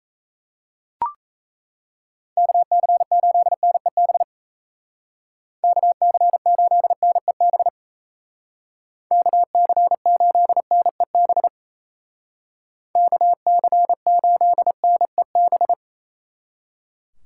Morse code, heard as a keyed tone, about 33 wpm, 700 Hz. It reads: KC8NEB KC8NEB KC8NEB KC8NEB
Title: Speed Racing - USA 2x3 Call Signs 28wpm